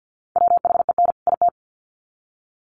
Morse code, heard as a keyed tone, about 50 wpm, 700 Hz. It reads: WHEN IT
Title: Sets of 2 Words 50wpm